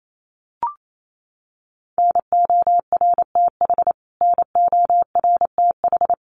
Morse code, heard as a keyed tone, about 28 wpm, 700 Hz. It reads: NORTH NORTH
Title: Top 500 Words Repeated 28wpm